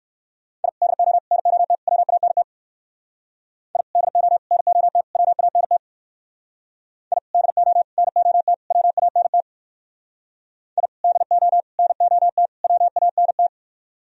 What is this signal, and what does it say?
Morse code, heard as a keyed tone, about 55 wpm, 700 Hz. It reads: I DO NOT WANT I DO NOT WANT I DO NOT WANT I DO NOT WANT
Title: Speed-Racing - Sets of 4 Words 45wpm